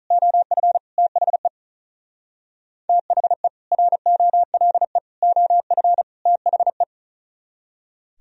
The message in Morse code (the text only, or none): OF THE THE ROLE OF THE